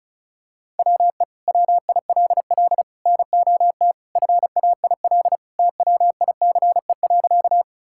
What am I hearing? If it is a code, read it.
Morse code, heard as a keyed tone, about 36 wpm, 700 Hz. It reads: WE WILL NOT FAIL TWICE.
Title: Sentences from Top 1600 Words 35wpm